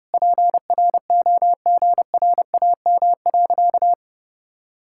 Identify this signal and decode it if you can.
Morse code, heard as a keyed tone, about 30 wpm, 700 Hz. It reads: PROGRAM.
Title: Sentences from Top 1000 Words - Review Part 5 - 30wpm